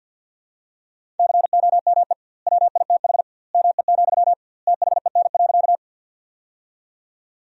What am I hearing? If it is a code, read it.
Morse code, heard as a keyed tone, about 46 wpm, 700 Hz. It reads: COME WITH ME, THEN.
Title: 2x Word-Spacing - Sentences from Top 200 Words - 50wpm